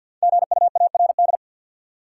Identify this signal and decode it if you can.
Morse code, heard as a keyed tone, about 49 wpm, 700 Hz. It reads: GUARD
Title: Sentences from Top 700 Words - New Word Drill - 50wpm